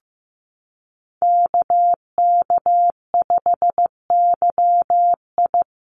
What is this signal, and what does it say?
Morse code, heard as a keyed tone, about 15 wpm, 700 Hz. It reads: KK5YI
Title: USA Call Signs Repeated 2x 15wpm